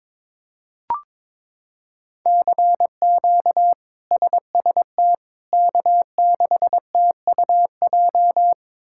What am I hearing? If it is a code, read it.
Morse code, heard as a keyed tone, about 22 wpm, 700 Hz. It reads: CQ SST K6TUJ
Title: K1USN SST QSO Practice - 22wpm